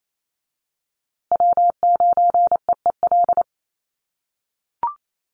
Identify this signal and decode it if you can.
Morse code, heard as a keyed tone, about 28 wpm, 700 Hz. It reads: W9EEL
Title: USA Call Signs Repeated 2x 28wpm